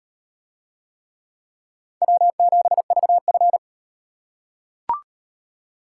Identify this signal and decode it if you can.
Morse code, heard as a keyed tone, about 38 wpm, 700 Hz. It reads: W7VF